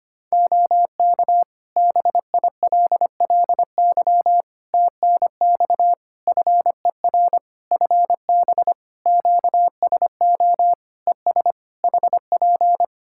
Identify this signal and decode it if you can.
Morse code, heard as a keyed tone, about 25 wpm, 700 Hz. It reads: OK BILLY TNX FER FB QSO ES HP
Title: Practice QSO - Exchange 4 - Ending - 25wpm